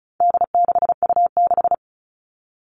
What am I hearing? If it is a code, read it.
Morse code, heard as a keyed tone, about 34 wpm, 700 Hz. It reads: D6U6
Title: All Cut Numbers - 4-Digit 35wpm